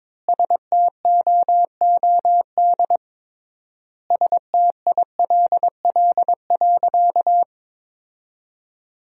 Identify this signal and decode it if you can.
Morse code, heard as a keyed tone, about 22 wpm, 700 Hz. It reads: STOOD STILL.
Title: 3x Word-Spacing - Sentences from Top 500 Words - 22wpm